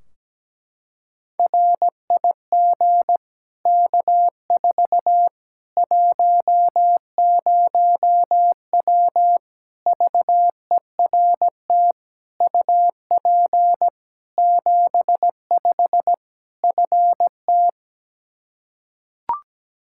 Morse code, heard as a keyed tone, about 17 wpm, 700 Hz. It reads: RIG K4 10W VERT UP 75 FT
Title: Practice QSO - Exchange 2 - Info - 17wpm